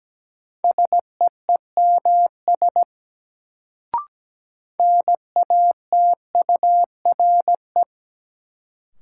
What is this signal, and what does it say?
Morse code, heard as a keyed tone, about 17 wpm, 700 Hz. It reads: SEEMS NATURE
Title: Sets of 3 Words - New Word Drill - 17wpm